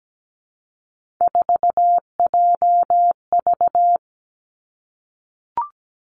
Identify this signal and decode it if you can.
Morse code, heard as a keyed tone, about 17 wpm, 700 Hz. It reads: 4JV
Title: Three Challenging Letter-Number Combinations 17wpm